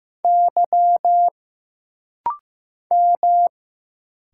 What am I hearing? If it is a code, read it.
Morse code, heard as a keyed tone, about 15 wpm, 700 Hz. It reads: Y M
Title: Single Letters - Rapid-Fire 15wpm